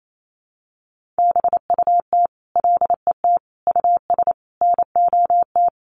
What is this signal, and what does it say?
Morse code, heard as a keyed tone, about 28 wpm, 700 Hz. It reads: BUT LET US NOT
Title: Sentences from Top 1500 Words 28wpm